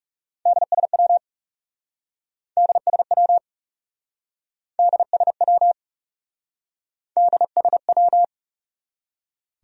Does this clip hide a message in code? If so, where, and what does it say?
Yes: Morse code from 0.5 to 8.2 s, heard as a keyed tone, about 40 wpm, 700 Hz. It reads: DSW DSW DSW DSW